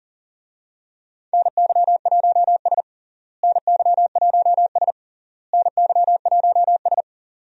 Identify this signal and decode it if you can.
Morse code, heard as a keyed tone, about 40 wpm, 700 Hz. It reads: NY1S NY1S NY1S
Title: USA Call Signs - 4 Characters - Repeated 3x - 40wpm